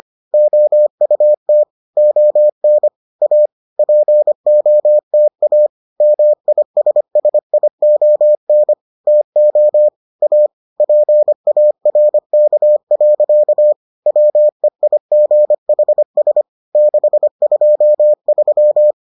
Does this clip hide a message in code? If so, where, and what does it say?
Yes: Morse code from 0.3 to 18.9 s, heard as a keyed tone, about 25 wpm, 600 Hz. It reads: OUT ON A POTA MISSION TO A PARK. WEIGHS 623